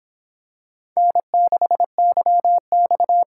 Morse code, heard as a keyed tone, about 26 wpm, 700 Hz. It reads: N6YX